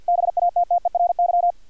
Morse code, heard as a keyed tone, about 50 wpm, 700 Hz, with radio noise. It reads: BATTERY
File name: archived / CRSnet